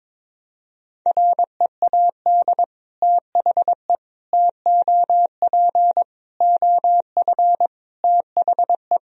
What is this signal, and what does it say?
Morse code, heard as a keyed tone, about 22 wpm, 700 Hz. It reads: READ THE TOP OF THE
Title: Sentences from Top 1000 Words - Review Part 2 - 22wpm